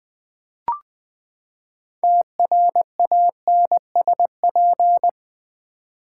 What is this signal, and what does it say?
Morse code, heard as a keyed tone, about 20 wpm, 700 Hz. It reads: TRANSP
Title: Build A Word - 6 Letters At A Time - 12-Letter Words - 20wpm